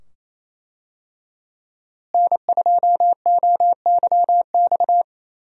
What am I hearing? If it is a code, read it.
Morse code, heard as a keyed tone, about 28 wpm, 700 Hz. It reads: N2OYX
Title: USA Call Signs - 5 Characters - Repeated 2x - 28wpm